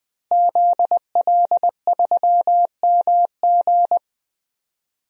Morse code, heard as a keyed tone, about 20 wpm, 700 Hz. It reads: ZL3MG
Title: DXCC-100 Call Signs - 5 Characters - Repeated 2x - 20wpm